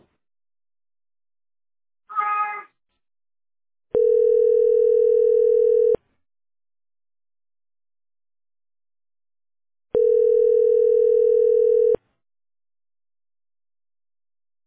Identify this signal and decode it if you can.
Ringback tone (North America), 440 + 480 Hz.